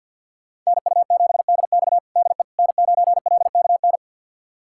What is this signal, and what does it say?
Morse code, heard as a keyed tone, about 51 wpm, 700 Hz. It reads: NU7DX DE N9LKN
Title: USA Callsign DE Callsign - 5 Characters - 50wpm